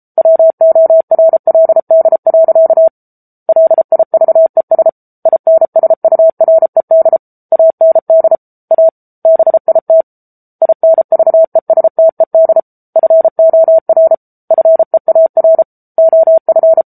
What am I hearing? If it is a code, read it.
Morse code, heard as a keyed tone, about 34 wpm, 650 Hz. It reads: WORLD. LIVES INSURED AND A BIT INVESTED FOR FEAR OF
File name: Book Two / 07-Chapter 7 - 33 WPM